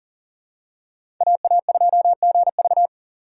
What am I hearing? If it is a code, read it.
Morse code, heard as a keyed tone, about 40 wpm, 700 Hz. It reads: AA2GV
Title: USA Call Signs - 5 Characters - 2x Character Spacing - 40wpm